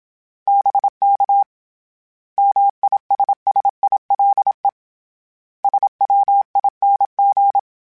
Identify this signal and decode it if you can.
Morse code, heard as a keyed tone, about 27 wpm, 800 Hz. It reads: BK MISSILE SWING
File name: Misc / Misc-mp3s / Words @25